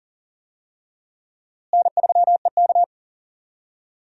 Morse code, heard as a keyed tone, about 39 wpm, 700 Hz. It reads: N3EK
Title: USA Call Signs - 4 Characters - Repeated 2x - 40wpm